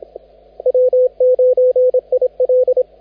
Morse code, heard as a keyed tone, about 27 wpm, 510 Hz, with radio noise. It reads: W9IL
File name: cqwwcw2015audio / log / W9IL de DH8BQA CQWWCW CW 21MHz